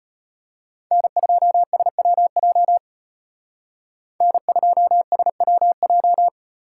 Morse code, heard as a keyed tone, about 31 wpm, 700 Hz. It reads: N2SWJ N2SWJ